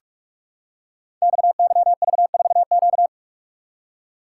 Morse code, heard as a keyed tone, about 44 wpm, 700 Hz. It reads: XYV4Q
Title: Code Groups - 5 Letter:Numbers - 45wpm